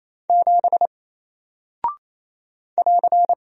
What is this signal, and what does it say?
Morse code, heard as a keyed tone, about 28 wpm, 700 Hz. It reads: 7 <AR>
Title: Single Character (Letter:Number:Punctuation:Prosign) Alt 28wpm